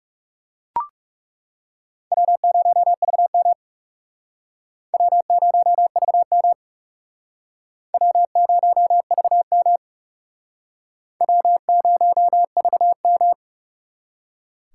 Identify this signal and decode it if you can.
Morse code, heard as a keyed tone, about 38 wpm, 700 Hz. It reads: W0VM W0VM W0VM W0VM